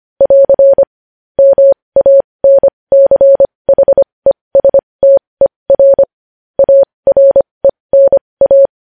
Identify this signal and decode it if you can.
Morse code, heard as a keyed tone, about 25 wpm, spaced as for 20 wpm, 550 Hz. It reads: <AR> MANCHESTER ARENA
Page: news-headlines-in-morse-code-at-20-wpm